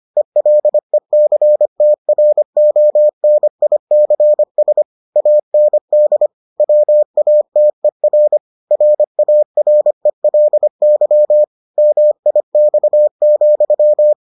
Morse code, heard as a keyed tone, about 25 wpm, 600 Hz. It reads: ELECTRONICS AND WATER RARELY MIX,